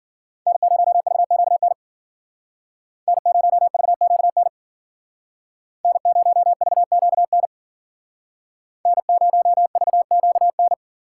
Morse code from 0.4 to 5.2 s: N0VQN N0VQN

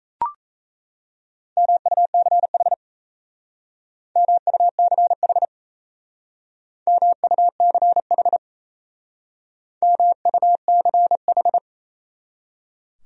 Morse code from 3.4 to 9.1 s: MUCH MUCH

